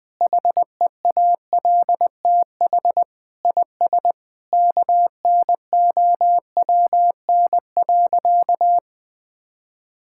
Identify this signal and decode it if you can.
Morse code, heard as a keyed tone, about 20 wpm, 700 Hz. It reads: HEALTH IS KNOWN.
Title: Sentences with 3-Word Phrases - 20wpm